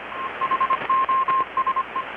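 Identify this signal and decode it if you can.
Morse code, heard as a keyed tone, about 25 wpm, 1060 Hz, with radio noise. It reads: HOSE
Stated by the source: Morse from KPH's Night of Nights XX maritime radiotelegraphy event, received in Chiapas, Mexico about 3,800 km away on a Kenwood R-600 with a grounded 10 m long-wire antenna.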